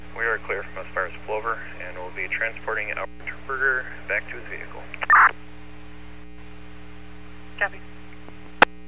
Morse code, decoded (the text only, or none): none